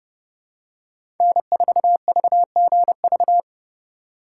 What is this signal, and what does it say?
Morse code, heard as a keyed tone, about 32 wpm, 700 Hz. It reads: N4VGV
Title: USA Call Signs - 5 Characters - 3x Character Spacing - 30wpm